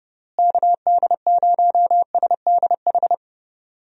Morse code, heard as a keyed tone, about 30 wpm, 700 Hz. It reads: KD0SDH
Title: USA Call Signs - 6 Characters - Repeated 3x - 30wpm